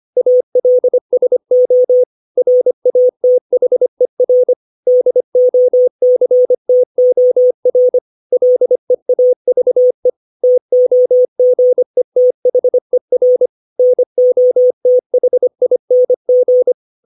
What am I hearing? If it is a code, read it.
Morse code, heard as a keyed tone, about 25 wpm, 500 Hz. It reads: ALSO RATHER DOCTOR LEAVE TOGETHER NOTHING
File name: 632 words / 632 Words 25WPM